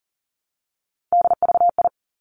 Morse code, heard as a keyed tone, about 38 wpm, 700 Hz. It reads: DVI
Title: Three Challenging Letter-Number Combinations 40wpm